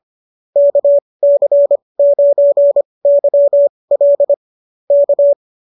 Morse code, heard as a keyed tone, about 25 wpm, spaced as for 20 wpm, 590 Hz. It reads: KC9YL K